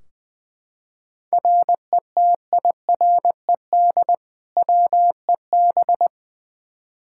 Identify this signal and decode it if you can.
Morse code, heard as a keyed tone, about 21 wpm, 700 Hz. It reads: RETIRED WEB
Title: K1USN SST QSO Practice 20wpm